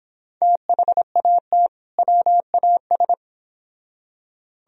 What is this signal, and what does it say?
Morse code, heard as a keyed tone, about 31 wpm, 700 Hz. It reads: THAT WAS